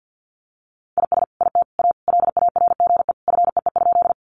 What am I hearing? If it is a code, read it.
Morse code, heard as a keyed tone, about 50 wpm, 700 Hz. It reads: IS IT A LARGE FEE?
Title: Sentences from Top 1500 Words 50wpm